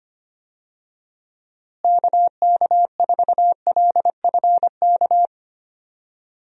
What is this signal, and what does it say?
Morse code, heard as a keyed tone, about 25 wpm, 700 Hz. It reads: KK4LFK
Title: USA Call Signs - 6 Characters - Repeated 3x - 25wpm